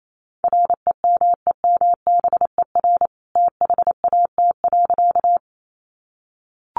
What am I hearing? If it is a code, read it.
Morse code, heard as a keyed tone, about 28 wpm, 700 Hz. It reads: REMEMBER THAT.
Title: A Sound Of Thunder 28wpm